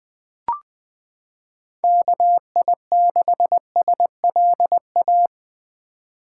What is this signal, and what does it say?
Morse code, heard as a keyed tone, about 20 wpm, 700 Hz. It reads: KI6SLA
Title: 2023 SOTA Chaser Call Signs - 20wpm